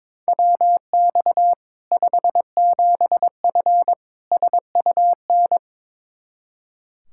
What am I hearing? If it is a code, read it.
Morse code, heard as a keyed tone, about 22 wpm, 700 Hz. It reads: WX 57F SUN
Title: K1USN SST QSO Practice 22wpm